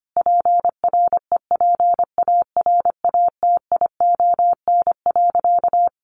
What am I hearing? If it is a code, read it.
Morse code, heard as a keyed tone, about 25 wpm, 700 Hz. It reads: PREPARATION.
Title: Sentences from Top 1400 Words 25wpm